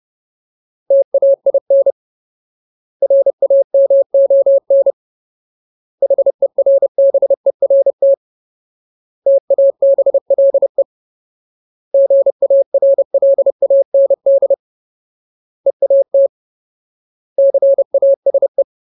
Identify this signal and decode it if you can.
Morse code, heard as a keyed tone, about 30 wpm, 550 Hz. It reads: TAIN RAMON HERBERT TABLE GARLAND EAT CASE
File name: Ultimate Word List / 30WPM Ultimate